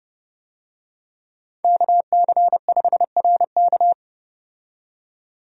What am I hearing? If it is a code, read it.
Morse code, heard as a keyed tone, about 30 wpm, 700 Hz. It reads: KC5RK
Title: USA Call Signs - 5 Characters - 30wpm